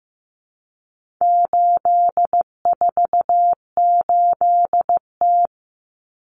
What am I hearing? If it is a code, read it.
Morse code, heard as a keyed tone, about 15 wpm, 700 Hz. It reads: 848T
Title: Common Cut Numbers - 4-Digit 15wpm